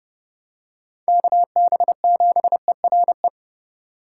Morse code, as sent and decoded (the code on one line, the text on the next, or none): -.- -... --... .-. .
KB7ERE